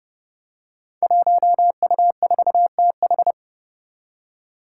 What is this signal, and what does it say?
Morse code, heard as a keyed tone, about 28 wpm, 700 Hz. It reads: 1U4TH
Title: Five Challenging Letter-Number Combinations 30wpm